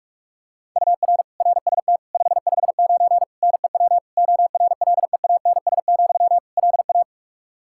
Morse code, heard as a keyed tone, about 46 wpm, 700 Hz. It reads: UR RST 559 NEW ORLEANS, LA
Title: Practice QSO - Exchange 1 - Beginning - 45wpm